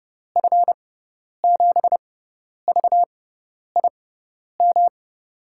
Morse code, heard as a keyed tone, about 30 wpm, spaced as for 10 wpm, 700 Hz. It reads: F7VIM
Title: USA Call Signs - 6 Characters - 6x Character Spacing - 30wpm